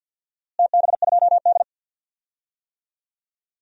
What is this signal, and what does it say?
Morse code, heard as a keyed tone, about 52 wpm, 700 Hz. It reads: TB2D
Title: DXCC-100 Call Signs - 4 Characters - Repeated 3x - 50wpm